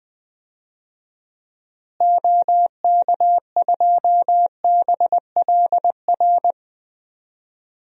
Morse code, heard as a keyed tone, about 20 wpm, 700 Hz. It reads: OK2BLR
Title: DXCC-100 Call Signs - 6 Characters - Repeated 3x - 20wpm